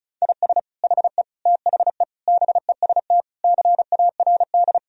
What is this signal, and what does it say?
Morse code, heard as a keyed tone, about 36 wpm, 700 Hz. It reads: IS HE THE BEST CARD
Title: Sentences from Top 1000 Words - Review Part 4 - 35wpm